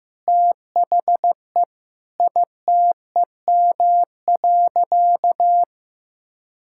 Morse code, heard as a keyed tone, about 15 wpm, 700 Hz. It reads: THE ITEM.
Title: Sentences from Top 1000 Words - Review Part 6 - 15wpm